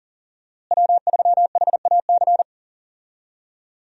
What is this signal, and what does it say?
Morse code, heard as a keyed tone, about 40 wpm, 700 Hz. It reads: W3HAC
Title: USA Call Signs - 5 Characters - Repeated 3x - 40wpm